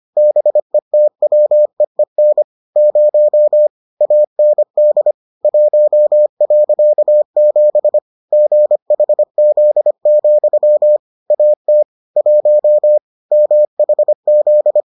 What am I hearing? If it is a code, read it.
Morse code, heard as a keyed tone, about 25 wpm, 600 Hz. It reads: BETWEEN 0 AND 1.7 GHZ, AT 1 MHZ